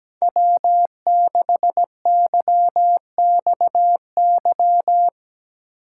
Morse code, heard as a keyed tone, about 17 wpm, 700 Hz. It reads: W6YXY
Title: USA Call Signs - 2x Character Spacing - 17wpm